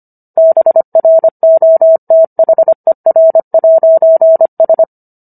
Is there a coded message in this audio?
Yes: Morse code from 0.4 to 4.8 s, heard as a keyed tone, about 25 wpm, 650 Hz. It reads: BROTHER'S